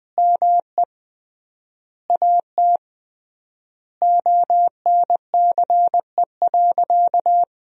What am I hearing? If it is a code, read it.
Morse code, heard as a keyed tone, about 21 wpm, 700 Hz. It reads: ME AT ONCE.
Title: 3x Word-Spacing - Sentences from Top 500 Words - 20wpm